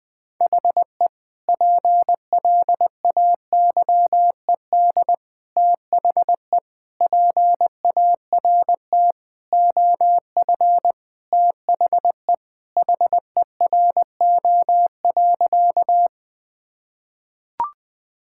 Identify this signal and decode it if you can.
Morse code, heard as a keyed tone, about 21 wpm, 700 Hz. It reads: HE PLAYED THE PART OF THE HERO.
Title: Sentences with 4-Word Phrases - Spoken Beginning - 20wpm